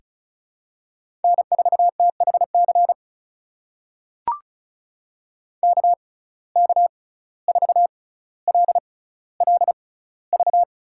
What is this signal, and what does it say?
Morse code, heard as a keyed tone, about 36 wpm, spaced as for 17 wpm, 700 Hz. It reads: N4THC KK4LLV
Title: USA Call Signs - 6x Character Spacing - 35wpm